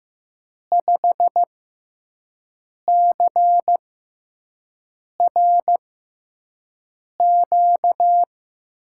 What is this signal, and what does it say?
Morse code, heard as a keyed tone, about 15 wpm, spaced as for 5 wpm, 700 Hz. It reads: 5CRQ
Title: USA Call Signs - 6 Characters - 6x Character Spacing - 15wpm